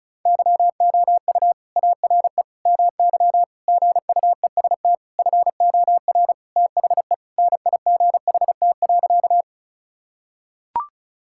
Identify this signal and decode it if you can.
Morse code, heard as a keyed tone, about 36 wpm, 700 Hz. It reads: YOU ARE MY GUEST FOR THE NIGHT.
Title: Sentences from Top 1600 Words 35wpm